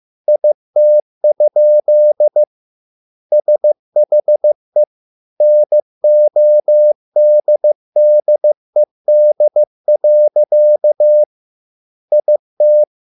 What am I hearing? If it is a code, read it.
Morse code, heard as a keyed tone, about 15 wpm, 600 Hz. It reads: IT? SHE NODDED. IT